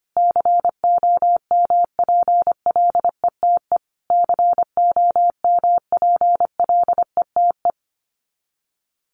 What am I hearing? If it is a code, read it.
Morse code, heard as a keyed tone, about 25 wpm, 700 Hz. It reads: COMPLETE COMPLETE
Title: Top 500 Words Repeated 25wpm